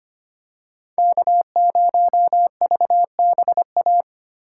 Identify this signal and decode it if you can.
Morse code, heard as a keyed tone, about 25 wpm, 700 Hz. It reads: K0VBA
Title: USA Call Signs - 4x Character Spacing - 25wpm